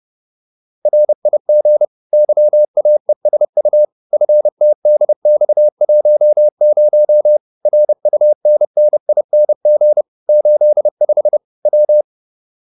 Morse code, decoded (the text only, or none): RIG YAESU FTDX10 RUNNING 85 W